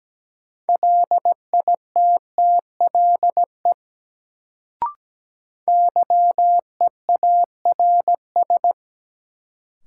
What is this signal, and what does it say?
Morse code, heard as a keyed tone, about 17 wpm, 700 Hz. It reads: LITTLE YEARS
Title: Sentences from Top 200 Words - New Word Drill - 17wpm